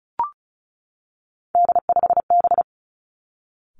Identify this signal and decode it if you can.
Morse code, heard as a keyed tone, about 36 wpm, 700 Hz. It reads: D5B
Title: Three Challenging Letter-Number Combinations 35wpm